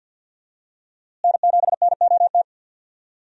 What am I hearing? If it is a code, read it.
Morse code, heard as a keyed tone, about 46 wpm, 700 Hz. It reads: N7NOT
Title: USA Call Signs - 5 Characters - Repeated 3x - 50wpm